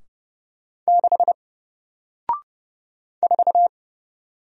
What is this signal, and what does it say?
Morse code, heard as a keyed tone, about 30 wpm, 700 Hz. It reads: 6 4
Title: Single Character (Letter:Number:Punctuation:Prosign) Alt 30wpm